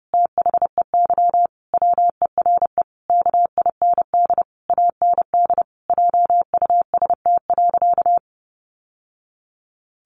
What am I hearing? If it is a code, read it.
Morse code, heard as a keyed tone, about 30 wpm, 700 Hz. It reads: THEY WERE KIND AND JUST.
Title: Sentences from Top 800 Words 30wpm